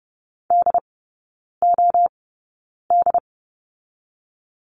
Morse code, heard as a keyed tone, about 30 wpm, spaced as for 9 wpm, 700 Hz. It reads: DOD
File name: Lesson 16e - Letter c - Three Characters 30wpm